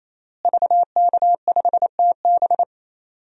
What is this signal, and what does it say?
Morse code, heard as a keyed tone, about 28 wpm, 700 Hz. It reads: VK5TB
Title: DXCC-100 Call Signs - 5 Characters - Repeated 3x - 28wpm